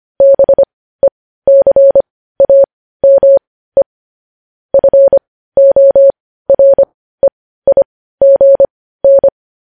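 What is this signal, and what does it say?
Morse code, heard as a keyed tone, about 25 wpm, spaced as for 15 wpm, 550 Hz. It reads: BECAME FOREIGN
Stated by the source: Morse news headlines sent at 15 words per minute.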